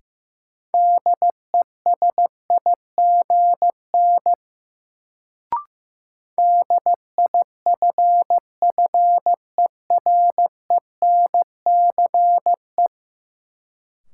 Morse code, heard as a keyed tone, about 15 wpm, 700 Hz. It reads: DESIGN DIFFERENCE
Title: Sentences from Top 800 Words - New Word Drill - 15wpm